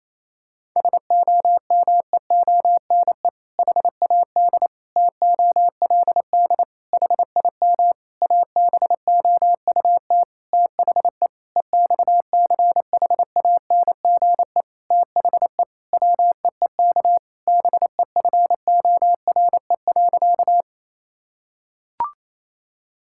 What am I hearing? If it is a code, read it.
Morse code, heard as a keyed tone, about 28 wpm, 700 Hz. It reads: SOMEONE HAD TOLD HIM ABOUT THE EXCHANGE THE WEEK BEFORE.
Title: Sentences from Top 1000 Words - Review Part 6 - 28wpm